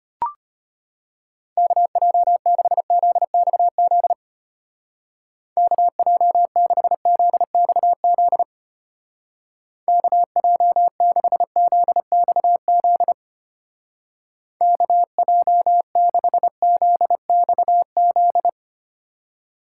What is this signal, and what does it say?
Morse code, heard as a keyed tone, about 30 wpm, 700 Hz. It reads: KJ6ZXZ KJ6ZXZ KJ6ZXZ KJ6ZXZ
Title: Speed Racing - USA 2x3 Call Signs 25wpm